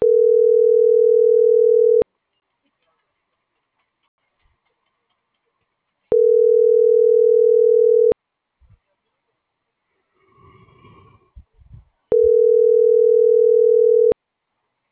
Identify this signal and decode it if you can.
Ringback tone (North America), 440 + 480 Hz.